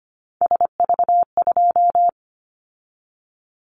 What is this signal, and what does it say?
Morse code, heard as a keyed tone, about 25 wpm, 700 Hz. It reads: SV2
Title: Callsign Prefixes for Top 50 DXCC - 25wpm